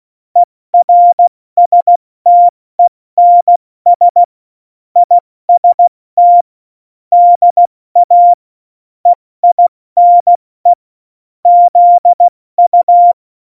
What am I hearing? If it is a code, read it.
Morse code, heard as a keyed tone, about 16 wpm, 700 Hz. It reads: ERSTENS IST DA EINE ZU
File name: Morsekurs / Witze / Witz 26. Januar 2021, 70 F80